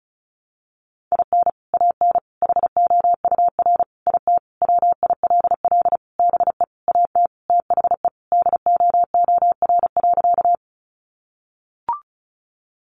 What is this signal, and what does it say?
Morse code, heard as a keyed tone, about 34 wpm, 700 Hz. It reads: IN AN HOUR IT WILL BE AT THE DOOR.
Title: Sentences from Top 400 Words 35wpm